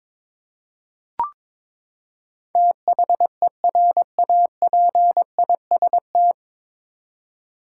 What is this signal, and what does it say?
Morse code, heard as a keyed tone, about 22 wpm, 700 Hz. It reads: THERAPIST
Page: Morse Code Ninja practice 22wpm